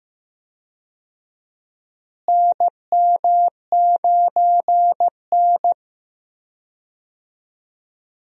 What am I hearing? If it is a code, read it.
Morse code, heard as a keyed tone, about 15 wpm, 700 Hz. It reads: NM9N
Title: USA Call Signs - 4 Characters - Repeated 2x - 15wpm